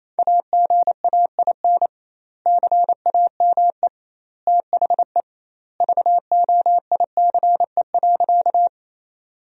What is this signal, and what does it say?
Morse code, heard as a keyed tone, about 28 wpm, 700 Hz. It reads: AGAIN CAME THE VOICE.
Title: 2x Word-Spacing - Sentences from Top 500 Words - 28wpm